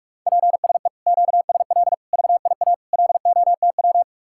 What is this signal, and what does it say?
Morse code, heard as a keyed tone, about 56 wpm, 700 Hz. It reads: PSE QSL VIA LOTW
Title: Speed-Racing Common QSO Phrases Encore - 45wpm